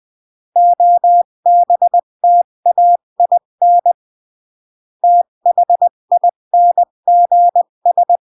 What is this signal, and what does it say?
Morse code, heard as a keyed tone, about 20 wpm, 700 Hz. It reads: OBTAIN THINGS
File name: CW / jodorowsky / 82 commandments 20-15